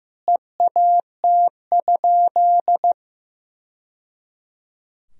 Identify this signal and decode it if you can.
Morse code, heard as a keyed tone, about 15 wpm, 700 Hz. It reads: EAT?
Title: Sentences from Top 1000 Words - Review Part 3 - 15wpm